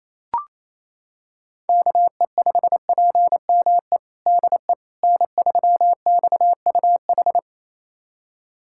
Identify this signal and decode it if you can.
Morse code, heard as a keyed tone, about 28 wpm, 700 Hz. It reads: KE5PME DE N3XUH
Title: USA Callsign DE Callsign 28wpm